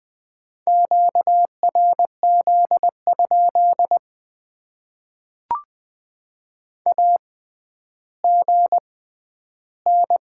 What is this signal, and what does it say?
Morse code, heard as a keyed tone, about 20 wpm, spaced as for 11 wpm, 700 Hz. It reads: QRZ? AGN
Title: Top 90 QSO Elements - 6x Character Spacing - 20wpm